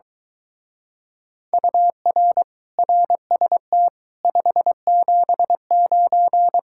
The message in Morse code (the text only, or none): UR RST 579